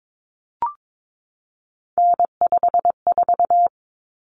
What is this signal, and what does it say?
Morse code, heard as a keyed tone, about 22 wpm, 700 Hz. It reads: N54
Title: Common Cut Numbers - 3-Digit 22wpm